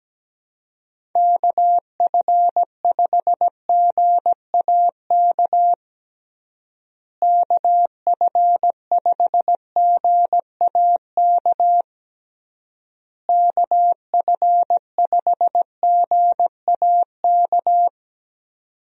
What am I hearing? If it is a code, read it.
Morse code, heard as a keyed tone, about 17 wpm, 700 Hz. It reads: KF5GAK KF5GAK KF5GAK
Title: USA Call Signs - 6 Characters - Repeated 3x - 17wpm